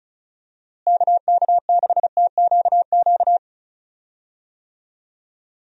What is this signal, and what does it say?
Morse code, heard as a keyed tone, about 34 wpm, 700 Hz. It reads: KK6TQQ
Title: USA Call Signs - 6 Characters - Repeated 3x - 35wpm